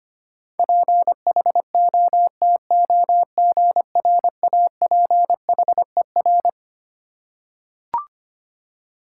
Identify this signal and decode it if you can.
Morse code, heard as a keyed tone, about 25 wpm, 700 Hz. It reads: PHOTOGRAPHER
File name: Twelve Letter Words - 25wpm